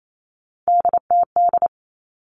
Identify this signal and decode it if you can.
Morse code, heard as a keyed tone, about 28 wpm, 700 Hz. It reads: DTD
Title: All Cut Numbers - 3-Digit 28wpm